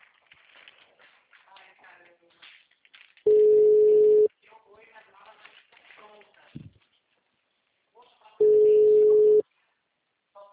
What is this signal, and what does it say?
Ringback tone (Europe, 425 Hz), with speech.